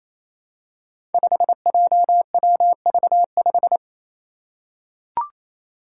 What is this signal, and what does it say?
Morse code, heard as a keyed tone, about 28 wpm, 700 Hz. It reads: 5JWV5
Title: Five Challenging Letter-Number Combinations 28wpm